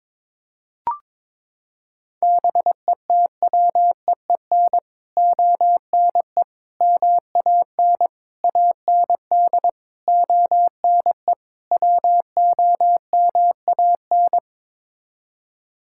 Morse code, heard as a keyed tone, about 22 wpm, 700 Hz. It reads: BETWEEN ONE MAN AND ONE WOMAN
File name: practice / 22wpm / Sets of 6 Words - 22wpm